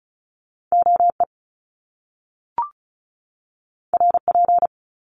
Morse code, heard as a keyed tone, about 36 wpm, 700 Hz. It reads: OE RP
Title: Top Two Letter Combinations in Words 35wpm